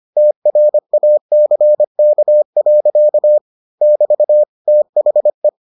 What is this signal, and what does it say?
Morse code, heard as a keyed tone, about 25 wpm, 600 Hz. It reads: TRACK. <BT> THE